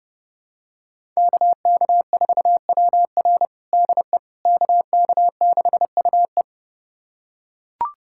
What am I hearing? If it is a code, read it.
Morse code, heard as a keyed tone, about 32 wpm, 700 Hz. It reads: KK4WR DE KK6UE
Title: USA Callsign DE Callsign - 5 Characters - 30wpm